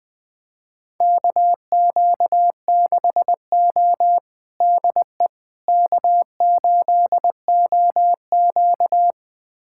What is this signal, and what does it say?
Morse code, heard as a keyed tone, about 21 wpm, 700 Hz. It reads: KQ6O DE K8OQ
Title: USA Callsign DE Callsign - 4 Characters - 20wpm